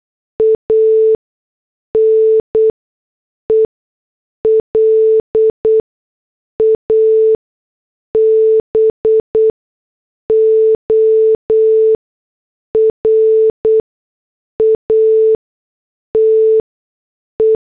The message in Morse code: AN ELABORATE